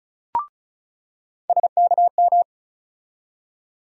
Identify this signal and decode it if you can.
Morse code, heard as a keyed tone, about 36 wpm, 700 Hz. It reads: SKM
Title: USA Call Sign Postfixes - 3 Letters 35wpm